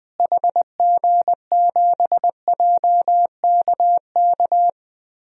Morse code, heard as a keyed tone, about 20 wpm, 700 Hz. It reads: HG7JKK
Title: DXCC-100 Call Signs - 6 Characters - Repeated 3x - 20wpm